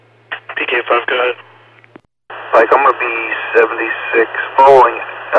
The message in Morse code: none

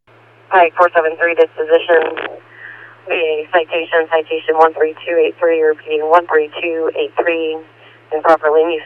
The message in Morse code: none

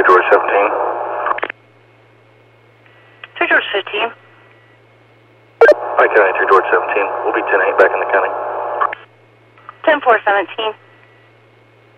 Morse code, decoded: none

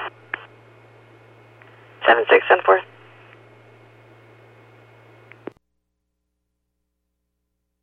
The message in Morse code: none

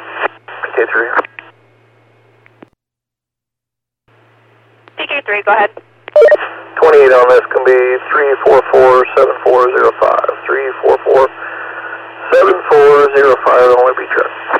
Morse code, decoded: none

